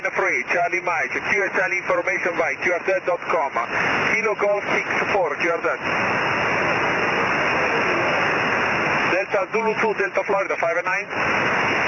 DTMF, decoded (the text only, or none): none